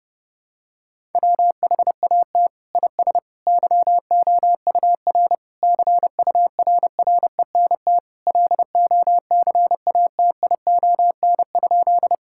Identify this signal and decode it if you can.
Morse code, heard as a keyed tone, about 30 wpm, 700 Hz. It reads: WHAT IS YOUR CURRENT LOCATION?
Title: Sentences from Top 1000 Words - Review Part 6 - 30wpm